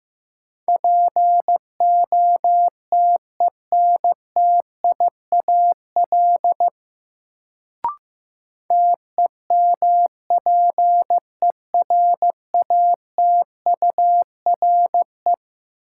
Morse code, heard as a keyed tone, about 16 wpm, 700 Hz. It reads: POTENTIAL TEMPERATURE
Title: Sentences from Top 1000 Words - Review Part 4 - New Word Drill - 15wpm